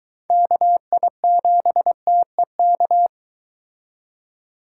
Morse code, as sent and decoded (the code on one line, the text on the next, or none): -.- .. --... - . -.-
KI7TEK